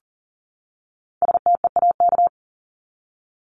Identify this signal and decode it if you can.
Morse code, heard as a keyed tone, about 40 wpm, 700 Hz. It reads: STEAK
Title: Top 1251-1500 Words 40wpm